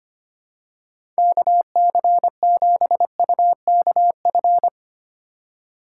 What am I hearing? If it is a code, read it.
Morse code, heard as a keyed tone, about 25 wpm, 700 Hz. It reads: KC7UKF
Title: USA Call Signs - 4x Character Spacing - 25wpm